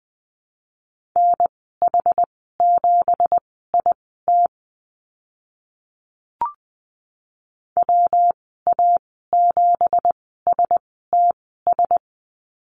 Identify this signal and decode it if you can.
Morse code, heard as a keyed tone, about 20 wpm, spaced as for 15 wpm, 700 Hz. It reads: NH7IT WA7STS
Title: 173 Lesson 28g - Number 9 - Call Signs 20wpm